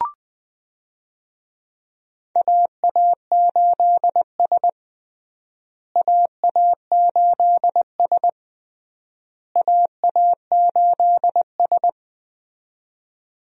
Morse code, heard as a keyed tone, about 20 wpm, 700 Hz. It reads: AA8S AA8S AA8S
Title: USA Call Signs - 4 Characters - Repeated 3x - 20wpm